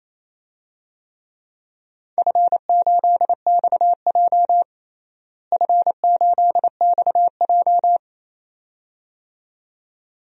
Morse code, heard as a keyed tone, about 28 wpm, 700 Hz. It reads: F8XJ F8XJ
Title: DXCC-100 Call Signs - 4 Characters - Repeated 2x - 28wpm